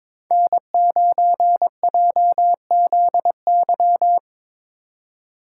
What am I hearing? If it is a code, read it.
Morse code, heard as a keyed tone, about 22 wpm, 700 Hz. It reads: N9JZY
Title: K1USN SST QSO Practice - 22wpm